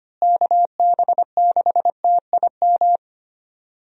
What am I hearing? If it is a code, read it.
Morse code, heard as a keyed tone, about 25 wpm, 700 Hz. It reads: KB6TIM
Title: USA Call Signs - 6 Characters - Repeated 3x - 25wpm